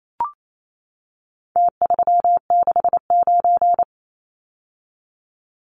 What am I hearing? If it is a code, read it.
Morse code, heard as a keyed tone, about 28 wpm, 700 Hz. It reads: T369
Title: Numbers - Contest Style - 4-Digit 28wpm